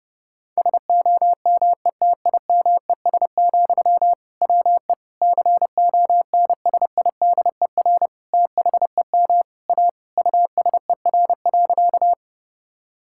Morse code, heard as a keyed tone, about 30 wpm, 700 Hz. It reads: SOMETIMES, WE CONSIDER THEM A USER.
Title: Sentences from Top 1000 Words - Review Part 5 - 30wpm